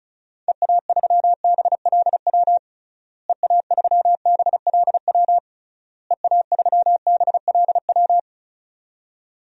Morse code, heard as a keyed tone, about 34 wpm, 700 Hz. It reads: EA3BLW EA3BLW EA3BLW
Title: DXCC-100 Call Signs - 6 Characters - Repeated 3x - 35wpm